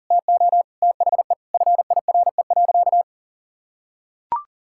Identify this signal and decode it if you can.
Morse code, heard as a keyed tone, about 40 wpm, 700 Hz. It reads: TO THE FIRE.
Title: Sentences from Top 1600 Words 40wpm